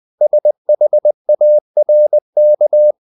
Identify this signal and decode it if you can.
Morse code, heard as a keyed tone, about 20 wpm, 600 Hz. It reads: SHARK
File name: the old man and the sea part 2 20wpm